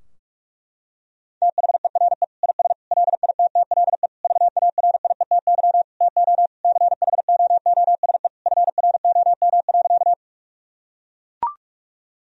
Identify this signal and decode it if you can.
Morse code, heard as a keyed tone, about 41 wpm, 700 Hz. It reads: THERE IS LITTLE VARIETY TO CHOOSE FROM.
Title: Sentences from Top 1000 Words - Review Part 4 - 45wpm